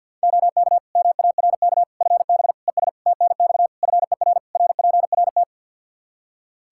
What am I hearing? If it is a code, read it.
Morse code, heard as a keyed tone, about 52 wpm, 700 Hz. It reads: OK MARK FB ES TNX FER RPRT
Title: Practice QSO - Exchange 2 - Info - 50wpm